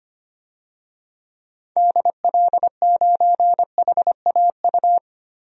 Morse code, heard as a keyed tone, about 25 wpm, 700 Hz. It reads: DL9HAU
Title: DXCC-100 Call Signs - 6 Characters - Repeated 3x - 25wpm